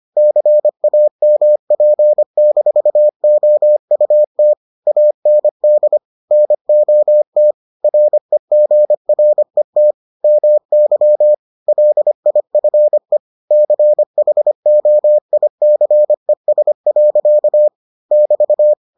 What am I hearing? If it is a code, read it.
Morse code, heard as a keyed tone, about 25 wpm, 600 Hz. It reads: CAMP-OUT AND NOT REGRET MY LIFE CHOICES. <BT>